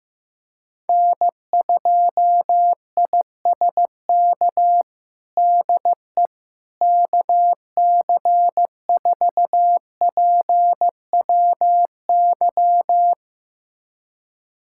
Morse code, heard as a keyed tone, about 15 wpm, 700 Hz. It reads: N2ISK DE KC4PWY
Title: USA Callsign DE Callsign 15wpm